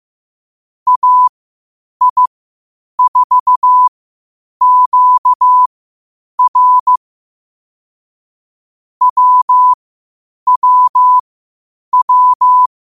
Morse code, heard as a keyed tone, about 15 wpm, spaced as for 8 wpm, 1000 Hz. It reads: AI4QR WWW
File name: qotdfiles / 8wpm